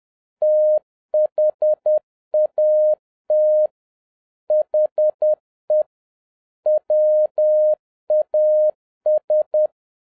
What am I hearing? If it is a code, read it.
Morse code, heard as a keyed tone, about 10 wpm, 610 Hz. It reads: THAT HE WAS
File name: 2011-07-mp3 / n36 THE 3.11 2 10wpm